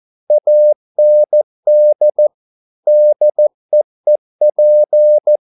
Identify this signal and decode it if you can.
Morse code, heard as a keyed tone, about 14 wpm, 600 Hz. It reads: AND DEEP